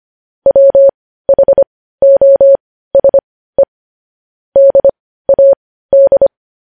Morse code, heard as a keyed tone, about 25 wpm, spaced as for 15 wpm, 550 Hz. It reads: WHOSE DAD
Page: news-headlines-in-morse-code-at-15-wpm